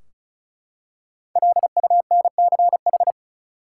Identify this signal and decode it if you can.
Morse code, heard as a keyed tone, about 36 wpm, 700 Hz. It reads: LUNCH
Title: Five Letter Words - 35wpm